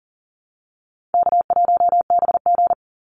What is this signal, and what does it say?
Morse code, heard as a keyed tone, about 40 wpm, 700 Hz. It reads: K1BG
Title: Easy Call Signs - 40wpm